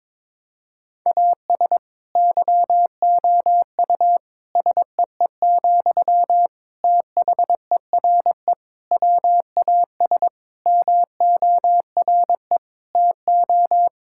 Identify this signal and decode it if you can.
Morse code, heard as a keyed tone, about 22 wpm, 700 Hz. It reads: AS YOU SEE, THERE WAS MORE TO
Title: Sentences from Top 1000 Words - Review Part 6 - 22wpm